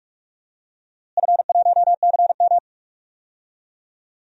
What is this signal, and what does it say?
Morse code, heard as a keyed tone, about 46 wpm, 700 Hz. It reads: F1CM
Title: DXCC-100 Call Signs - 4 Characters - Repeated 2x - 45wpm